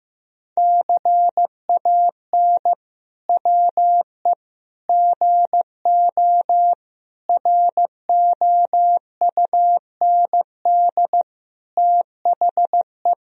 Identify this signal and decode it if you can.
Morse code, heard as a keyed tone, about 16 wpm, 700 Hz. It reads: CAN WE GO ROUND THE
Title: Sentences from Top 1000 Words - Review Part 1 - 15wpm